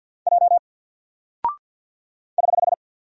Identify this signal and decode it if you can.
Morse code, heard as a keyed tone, about 48 wpm, 700 Hz. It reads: J <HH>
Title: Single Character (Letter:Number:Punctuation:Prosign) Alt 50wpm